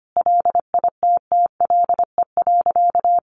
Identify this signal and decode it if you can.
Morse code, heard as a keyed tone, about 25 wpm, 700 Hz. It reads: LITTLE.
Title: Sentences from Top 1100 Words 25wpm